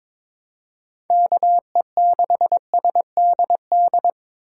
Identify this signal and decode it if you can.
Morse code, heard as a keyed tone, about 22 wpm, 700 Hz. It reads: KE6SDD
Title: USA Call Signs - 6 Characters - 6x Character Spacing - 22wpm